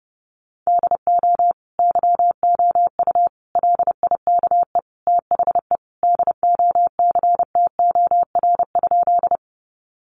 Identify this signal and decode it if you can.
Morse code, heard as a keyed tone, about 30 wpm, 700 Hz. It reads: DO YOU LIKE THE DOCTOR?
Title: Sentences from Top 1500 Words 30wpm